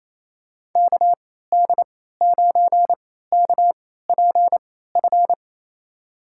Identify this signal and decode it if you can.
Morse code, heard as a keyed tone, about 27 wpm, spaced as for 16 wpm, 700 Hz. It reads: KD9KPF
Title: USA Call Signs - 6 Characters - 3x Character Spacing - 28wpm